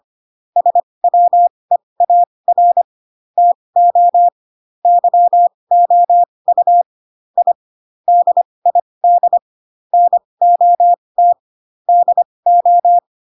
Morse code, heard as a keyed tone, about 25 wpm, spaced as for 20 wpm, 700 Hz. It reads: SWEAR TO YOU I DID NOT DO
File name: Pinocchio / Pinocchio ch2 15 25 ik1hge